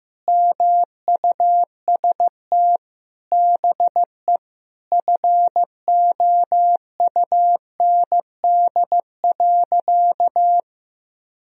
Morse code, heard as a keyed tone, about 15 wpm, 700 Hz. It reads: MUST BE FOUND.
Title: Sentences from Top 1000 Words - Review Part 4 - 15wpm